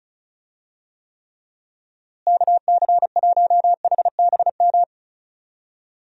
Morse code, heard as a keyed tone, about 34 wpm, 700 Hz. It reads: KC1HBM
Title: USA Call Signs - 6 Characters - Repeated 3x - 35wpm